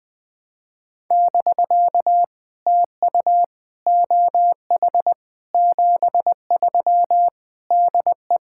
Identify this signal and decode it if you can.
Morse code, heard as a keyed tone, about 20 wpm, 700 Hz. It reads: <BK> TU OH 73 DE
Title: POTA QSO Practice - 20wpm